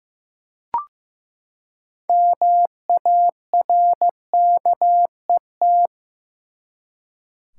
Morse code, heard as a keyed tone, about 15 wpm, 700 Hz. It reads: MARKET
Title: Six Letter Words - 15wpm